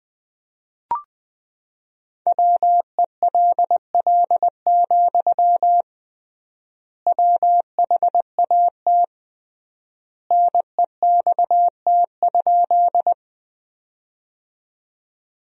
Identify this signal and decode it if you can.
Morse code, heard as a keyed tone, about 20 wpm, 700 Hz. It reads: WELL, WHAT NEXT?
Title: 3x Word-Spacing - Sentences from Top 300 Words - 20wpm